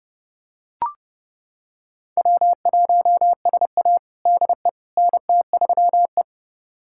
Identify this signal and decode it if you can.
Morse code, heard as a keyed tone, about 30 wpm, 700 Hz. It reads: W1SA DE NT3E
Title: USA Callsign DE Callsign - 4 Characters - 30wpm